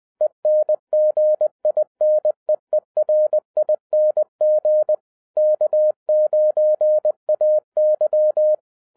Morse code, heard as a keyed tone, about 20 wpm, 610 Hz. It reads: ENGINEERING K9AY
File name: 2011-07-mp3 / n23 QRP OTAKU 20wpm